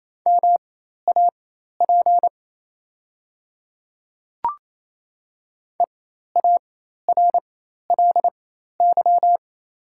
Morse code, heard as a keyed tone, about 28 wpm, spaced as for 13 wpm, 700 Hz. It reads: MAP EARLY
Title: 4x Character Spacing - Top 500 Words - 28wpm